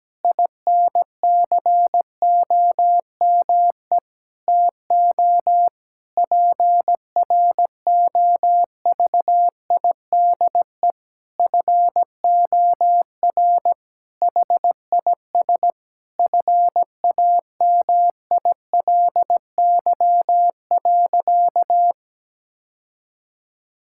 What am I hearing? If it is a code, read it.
Morse code, heard as a keyed tone, about 17 wpm, 700 Hz. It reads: INCOME TO PROVIDE FOR HIS FAMILY.
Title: Sentences from Top 1000 Words - Review Part 5 - 17wpm